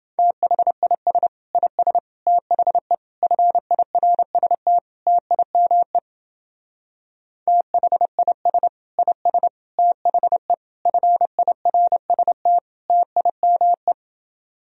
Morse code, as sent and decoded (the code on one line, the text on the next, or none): - .... .. ... / .. ... / - .... / ..-. .. .-. ... - / - .. -- . / - .... .. ... / .. ... / - .... / ..-. .. .-. ... - / - .. -- .
THIS IS THE FIRST TIME THIS IS THE FIRST TIME